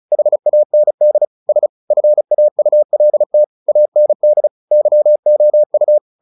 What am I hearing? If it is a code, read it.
Morse code, heard as a keyed tone, about 34 wpm, 600 Hz. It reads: HAND S FAULT AND YOU
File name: the old man and the sea / the old man and the sea 1 35wpm